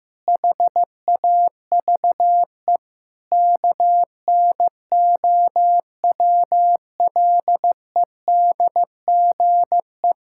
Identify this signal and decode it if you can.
Morse code, heard as a keyed tone, about 15 wpm, 700 Hz. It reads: HAVE KNOWLEDGE
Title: Sentences from Top 1000 Words - Review Part 6 - 15wpm